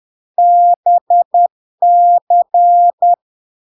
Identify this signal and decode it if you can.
Morse code, heard as a keyed tone, about 10 wpm, 700 Hz. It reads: BC
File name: LARK-FD-2022-Log-in-CW-10wpm-700-Hz